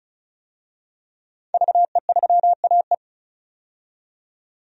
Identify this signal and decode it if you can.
Morse code, heard as a keyed tone, about 36 wpm, 700 Hz. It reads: VE3AE
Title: DXCC-100 Call Signs - 5 Characters - 35wpm